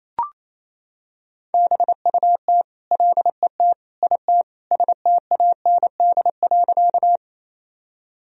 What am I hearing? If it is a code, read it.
Morse code, heard as a keyed tone, about 28 wpm, 700 Hz. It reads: BUT LET IT STAND.
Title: Sentences from Top 1000 Words - Review Part 1 - 28wpm